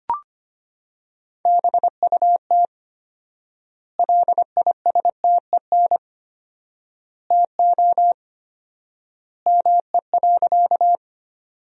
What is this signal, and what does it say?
Morse code, heard as a keyed tone, about 25 wpm, 700 Hz. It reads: BUT LISTEN TO ME.
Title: 4x Word-Spacing - Sentences from Top 400 Words - 25wpm